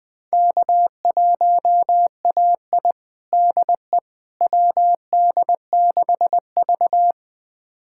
Morse code, heard as a keyed tone, about 20 wpm, 700 Hz. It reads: K1AI DE WD6V
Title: USA Callsign DE Callsign - 4 Characters - 20wpm